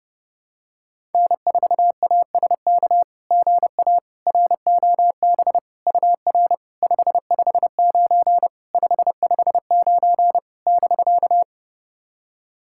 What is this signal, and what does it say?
Morse code, heard as a keyed tone, about 30 wpm, 690 Hz. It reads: N4ASK GA ROB UR 559 559 <BK>
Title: SOTA QSO Practice 30wpm